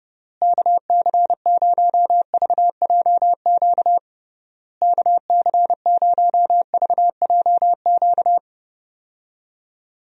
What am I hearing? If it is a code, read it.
Morse code, heard as a keyed tone, about 32 wpm, 700 Hz. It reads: KC0VJQ KC0VJQ
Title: USA Call Signs - 6 Characters - Repeated 3x - 30wpm